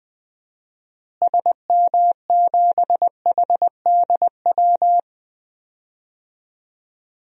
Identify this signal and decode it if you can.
Morse code, heard as a keyed tone, about 21 wpm, 700 Hz. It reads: SM7HDW
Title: DXCC-100 Call Signs - 6 Characters - Repeated 2x - 20wpm